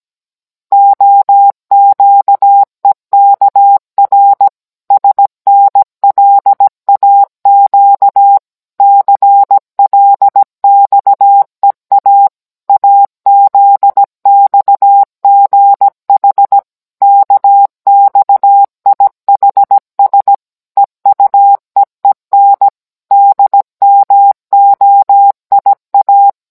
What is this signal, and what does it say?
Morse code, heard as a keyed tone, about 17 wpm, 800 Hz. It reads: OQEKR SNLAQ CLXEA AZXGH KXIHS EUEEN DMOIA